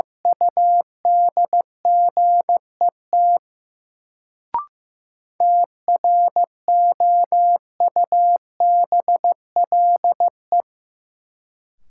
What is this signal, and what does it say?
Morse code, heard as a keyed tone, about 15 wpm, 700 Hz. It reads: UDGET TROUBLE
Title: Sentences from Top 1200 Words - New Word Drill Alt - 15wpm